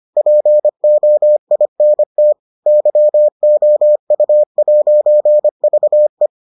Morse code, heard as a keyed tone, about 25 wpm, 600 Hz. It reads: POINT YOU'VE